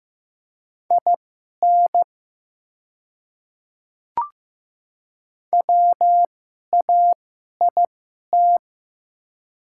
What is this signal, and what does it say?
Morse code, heard as a keyed tone, about 15 wpm, spaced as for 11 wpm, 690 Hz. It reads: IN WAIT
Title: 2x Character Spacing - Top 500 Words - 15wpm